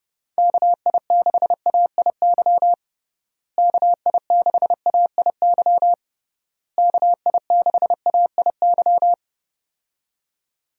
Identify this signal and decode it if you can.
Morse code, heard as a keyed tone, about 31 wpm, 700 Hz. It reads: KI6AIY KI6AIY KI6AIY
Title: USA Call Signs - 6 Characters - Repeated 3x - 30wpm